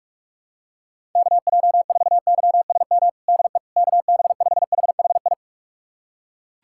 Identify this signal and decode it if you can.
Morse code, heard as a keyed tone, about 46 wpm, 700 Hz. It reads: KJ4YSM DE KB5HHI
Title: USA Callsign DE Callsign - 6 Characters - 45wpm